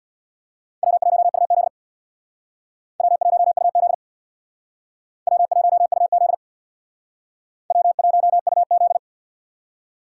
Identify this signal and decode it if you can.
Morse code, heard as a keyed tone, about 61 wpm, 700 Hz. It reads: W1UZ W1UZ W1UZ W1UZ
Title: Speed Racing - USA Call Signs 50wpm